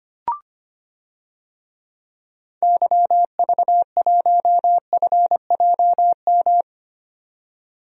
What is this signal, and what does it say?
Morse code, heard as a keyed tone, about 25 wpm, 700 Hz. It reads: YV1FJM
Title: DXCC-100 Call Signs - 6 Characters - 25wpm